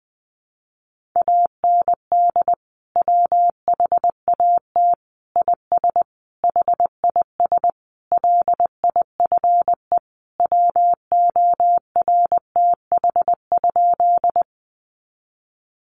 Morse code, heard as a keyed tone, about 20 wpm, 700 Hz. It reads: AND WHAT IS HIS LIFE WORTH?
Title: Sentences from Top 800 Words 20wpm